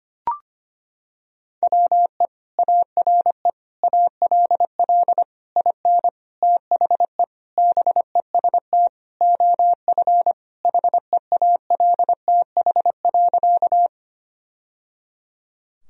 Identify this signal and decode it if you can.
Morse code, heard as a keyed tone, about 25 wpm, 700 Hz. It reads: WE ARE ALL IN THE BEST OF HEALTH.
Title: Sentences from Top 1000 Words - Review Part 3 - 25wpm